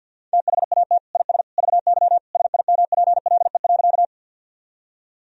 Morse code, heard as a keyed tone, about 52 wpm, 700 Hz. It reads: THAT IS VY SIMPLE.
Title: Sentences from Top 400 Words and Abbreviations 50wpm